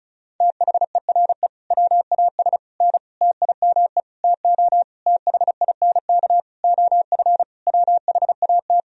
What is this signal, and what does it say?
Morse code, heard as a keyed tone, about 36 wpm, 700 Hz. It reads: THERE WAS N TIME TO THINK OF WHAT
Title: Sentences from Top 400 Words and Abbreviations 35wpm